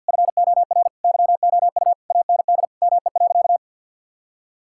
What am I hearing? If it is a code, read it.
Morse code, heard as a keyed tone, about 50 wpm, 700 Hz. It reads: FOR YOU AND ME.
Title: Sentences from Top 1600 Words 50wpm